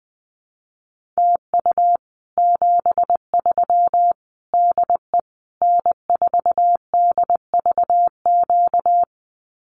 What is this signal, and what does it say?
Morse code, heard as a keyed tone, about 20 wpm, 700 Hz. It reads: TU 73 DE N4DVQ
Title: Common QSO Phrases Encore 20wpm